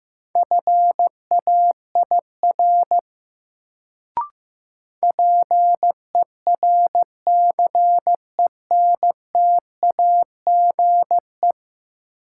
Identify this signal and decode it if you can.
Morse code, heard as a keyed tone, about 15 wpm, 700 Hz. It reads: FAIR PERCENTAGE
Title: Sentences from Top 1300 Words - New Word Drill Alt - 15wpm